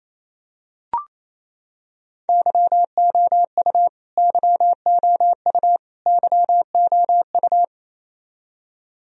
Morse code, heard as a keyed tone, about 28 wpm, 700 Hz. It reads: YOU YOU YOU
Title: Top 100 Words Repeated 28wpm